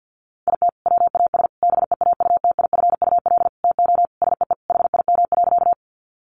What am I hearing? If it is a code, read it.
Morse code, heard as a keyed tone, about 50 wpm, 700 Hz. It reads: IT WAS BEAUTIFUL TO SEE HIM.
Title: Sentences from Top 900 Words 50wpm